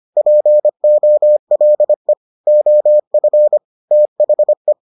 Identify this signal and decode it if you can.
Morse code, heard as a keyed tone, about 25 wpm, 600 Hz. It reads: POLE OF THE